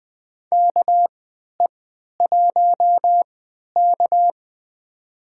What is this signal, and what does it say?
Morse code, heard as a keyed tone, about 20 wpm, spaced as for 11 wpm, 700 Hz. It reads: KE1K
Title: USA Call Signs - 4 Characters - 3x Character Spacing - 20wpm